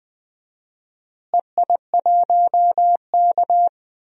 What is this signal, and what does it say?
Morse code, heard as a keyed tone, about 20 wpm, 700 Hz. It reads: EI1K
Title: DXCC-100 Call Signs - 4 Characters - Repeated 2x - 20wpm